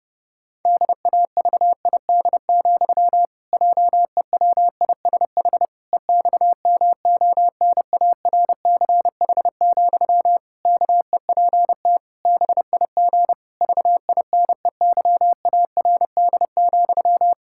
Morse code, heard as a keyed tone, about 30 wpm, 700 Hz. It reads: DAVID, JEWISH EXMONARCH, KEPT BIG VINEYARD,
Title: Pangram Challenge v2 - 30wpm